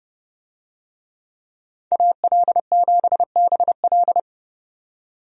Morse code, heard as a keyed tone, about 30 wpm, 700 Hz. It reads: AL7BL
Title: USA Call Signs - 5 Characters - 30wpm